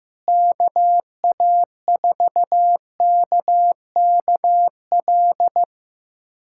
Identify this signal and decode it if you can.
Morse code, heard as a keyed tone, about 15 wpm, 700 Hz. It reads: KA4KKL
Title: USA Callsign DE Callsign - 6 Characters - 15wpm